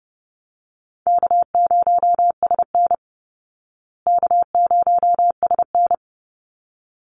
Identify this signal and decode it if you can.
Morse code, heard as a keyed tone, about 30 wpm, 700 Hz. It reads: K0SN K0SN
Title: Call Signs Repeated 3x - 30wpm